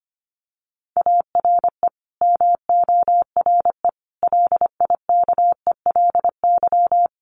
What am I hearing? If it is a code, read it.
Morse code, heard as a keyed tone, about 25 wpm, 700 Hz. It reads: ARE MORE LIKELY
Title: Sets of 4 Words 25wpm